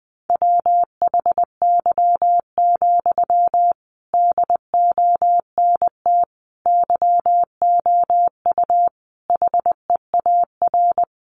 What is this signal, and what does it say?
Morse code, heard as a keyed tone, about 19 wpm, 700 Hz. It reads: WHY, DONT YOU HEAR
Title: Sentences from Top 300 Words 20wpm